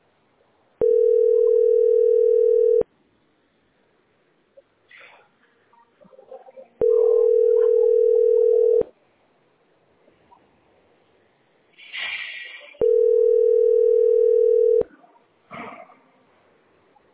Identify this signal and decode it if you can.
Ringback tone (North America), 440 + 480 Hz.